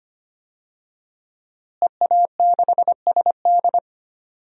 Morse code, heard as a keyed tone, about 25 wpm, 700 Hz. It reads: EA6SD
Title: DXCC-100 Call Signs - 5 Characters - Repeated 3x - 25wpm